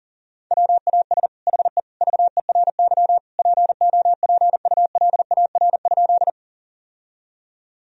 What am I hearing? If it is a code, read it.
Morse code, heard as a keyed tone, about 40 wpm, 700 Hz. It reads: WAS HE VERY POPULAR?
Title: Sentences from Top 1000 Words - Review Part 3 - 40wpm